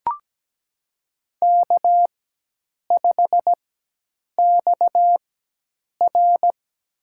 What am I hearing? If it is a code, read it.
Morse code, heard as a keyed tone, about 17 wpm, spaced as for 8 wpm, 700 Hz. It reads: K5XR
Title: USA Call Signs - 4 Characters - 4x Character Spacing - 17wpm